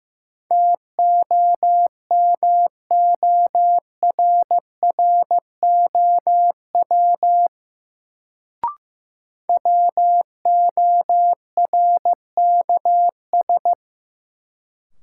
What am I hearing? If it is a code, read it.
Morse code, heard as a keyed tone, about 15 wpm, 700 Hz. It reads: TOMORROW WORKS
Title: Sentences from Top 1100 Words - New Word Drill Alt - 15wpm